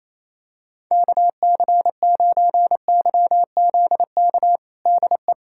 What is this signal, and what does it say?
Morse code, heard as a keyed tone, about 28 wpm, 700 Hz. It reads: KC9YZK DE
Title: USA Callsign DE Callsign 28wpm